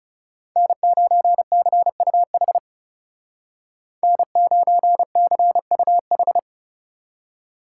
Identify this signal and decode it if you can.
Morse code, heard as a keyed tone, about 36 wpm, 700 Hz. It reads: N9CUH N9CUH
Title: Speed Racing - USA 1x3 Call Signs 30wpm